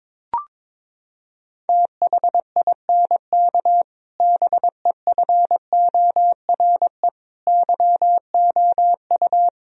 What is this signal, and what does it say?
Morse code, heard as a keyed tone, about 22 wpm, 700 Hz. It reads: THINK BEFORE YOU
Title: Sentences from Top 1000 Words - Review Part 1 - 22wpm